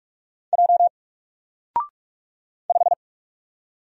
Morse code, heard as a keyed tone, about 44 wpm, 700 Hz. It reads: J 5
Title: Single Letter-Number Alt 45wpm